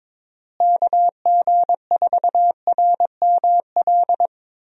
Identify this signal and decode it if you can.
Morse code, heard as a keyed tone, about 22 wpm, 700 Hz. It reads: KG4RML
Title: USA Call Signs - 6 Characters - Repeated 2x - 22wpm